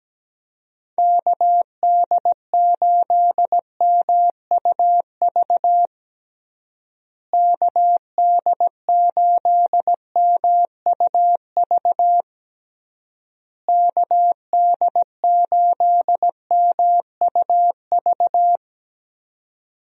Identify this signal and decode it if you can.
Morse code, heard as a keyed tone, about 17 wpm, 700 Hz. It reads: KD8MUV KD8MUV KD8MUV
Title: USA Call Signs - 6 Characters - Repeated 3x - 17wpm